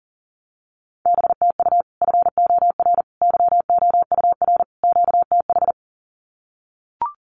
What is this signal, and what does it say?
Morse code, heard as a keyed tone, about 40 wpm, 700 Hz. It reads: BTU FOR YOUR QTH
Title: Common QSO Phrases 40wpm